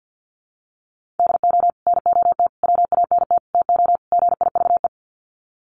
Morse code, heard as a keyed tone, about 50 wpm, 700 Hz. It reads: DO NOT WANT TO GIVE
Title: Sets of 5 Words Encore 50wpm